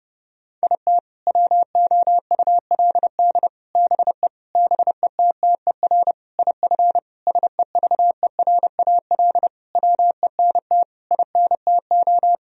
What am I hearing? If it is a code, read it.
Morse code, heard as a keyed tone, about 30 wpm, 700 Hz. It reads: IT WOULD BE BETTER IF SEVERAL WENT INTO
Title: Sentences from Top 1000 Words - Review Part 2 - 30wpm